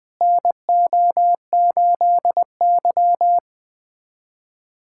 Morse code, heard as a keyed tone, about 20 wpm, 700 Hz. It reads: NO8Y
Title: USA Callsign DE Callsign - 4 Characters - 20wpm